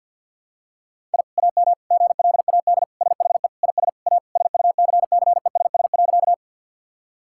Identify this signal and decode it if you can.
Morse code, heard as a keyed tone, about 52 wpm, 700 Hz. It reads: I AM GLAD SHE IS A SUCCESS.
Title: Sentences from Top 1000 Words - Review Part 6 - 50wpm